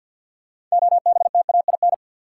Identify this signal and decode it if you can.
Morse code, heard as a keyed tone, about 53 wpm, 700 Hz. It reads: OBTAIN
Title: Sentences from Top 1300 Words - New Word Drill Alt - 50wpm